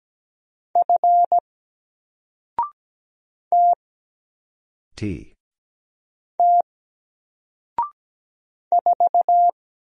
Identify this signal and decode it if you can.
Morse code, heard as a keyed tone, about 17 wpm, 700 Hz. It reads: F T T 4
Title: Single Letter-Number Alt 17wpm